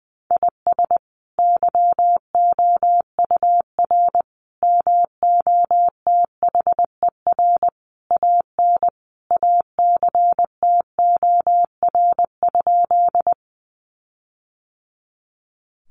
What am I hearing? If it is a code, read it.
Morse code, heard as a keyed tone, about 19 wpm, 700 Hz. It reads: IS YOUR MOTHER AN ACTOR?
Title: Sentences from Top 1500 Words 20wpm